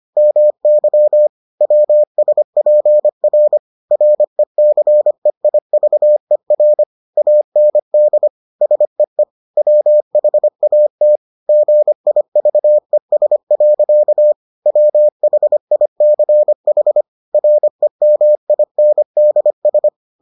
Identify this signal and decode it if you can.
Morse code, heard as a keyed tone, about 25 wpm, 600 Hz. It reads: MY WSPR RECEIVER AND SEE WHAT GIVES. WHICH REMINDS